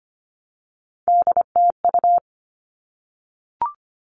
Morse code, heard as a keyed tone, about 25 wpm, 700 Hz. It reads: DTU
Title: All Cut Numbers - 3-Digit 25wpm